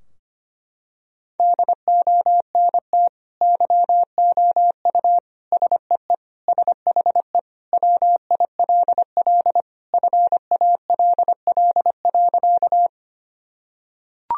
Morse code, heard as a keyed tone, about 25 wpm, 700 Hz. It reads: DONT YOU SEE SHE WILL FALL.
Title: Sentences from Top 1000 Words - Review Part 2 - 25wpm